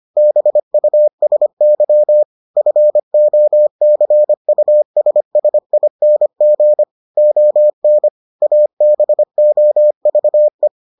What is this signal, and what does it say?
Morse code, heard as a keyed tone, about 25 wpm, 600 Hz. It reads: BUSY FOCUSSING ON ABOVE